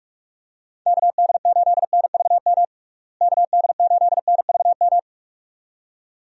Morse code, heard as a keyed tone, about 41 wpm, 700 Hz. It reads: KD8NVM KD8NVM
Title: USA Call Signs - 6 Characters - Repeated 2x - 45wpm